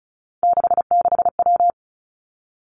Morse code, heard as a keyed tone, about 36 wpm, 700 Hz. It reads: 66W
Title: Three Challenging Letter-Number Combinations 35wpm